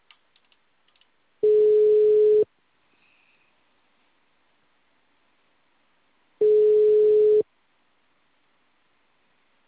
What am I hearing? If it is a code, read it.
Ringback tone (Europe, 425 Hz).
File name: out- -2015- - - .192